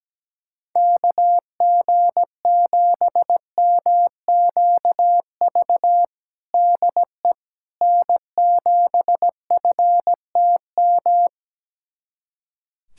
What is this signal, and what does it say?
Morse code, heard as a keyed tone, about 17 wpm, 700 Hz. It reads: KG7MQV DE N7FTM
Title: USA Callsign DE Callsign 17wpm